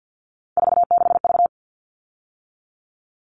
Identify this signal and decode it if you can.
Morse code, heard as a keyed tone, about 50 wpm, 700 Hz. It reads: 46V
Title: All Cut Numbers - 3-Digit 50wpm